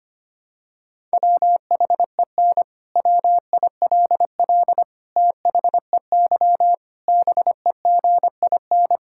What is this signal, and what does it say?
Morse code, heard as a keyed tone, about 25 wpm, 700 Hz. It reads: WHEN WILL THEY BEGIN
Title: Sentences from Top 1000 Words - Review Part 6 - 25wpm